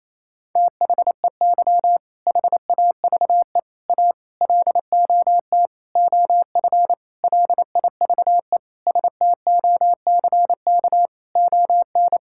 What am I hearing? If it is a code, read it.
Morse code, heard as a keyed tone, about 28 wpm, 700 Hz. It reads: THEY HAVE A LOT OF LIVE STOCK ON
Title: Sentences from Top 1000 Words - Review Part 5 - 28wpm